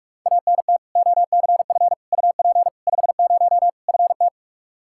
Morse code, heard as a keyed tone, about 41 wpm, 700 Hz. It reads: ANT OCF UP 50 FT
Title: Practice QSO - Exchange 2 - Info - 45wpm